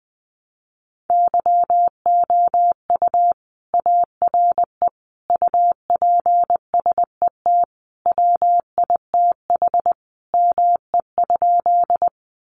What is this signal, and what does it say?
Morse code, heard as a keyed tone, about 20 wpm, 700 Hz. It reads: YOU ARE UPSET WITH ME?
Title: Sentences from Top 800 Words 20wpm